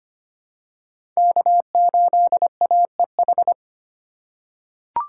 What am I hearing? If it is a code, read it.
Morse code, heard as a keyed tone, about 25 wpm, 700 Hz. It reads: K8AEH
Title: USA Call Signs - 5 Characters - 2x Character Spacing - 25wpm